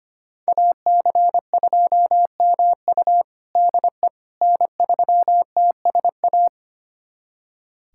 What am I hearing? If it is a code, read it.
Morse code, heard as a keyed tone, about 25 wpm, 700 Hz. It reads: AC2MU DE N3TSA
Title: USA Callsign DE Callsign - 5 Characters - 25wpm